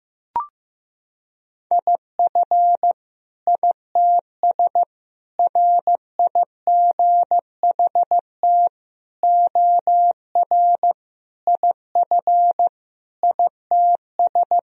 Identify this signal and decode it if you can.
Morse code, heard as a keyed tone, about 15 wpm, 700 Hz. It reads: IF ITS RIGHT OR IF ITS
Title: Sentences from Top 1000 Words - Review Part 3 - 15wpm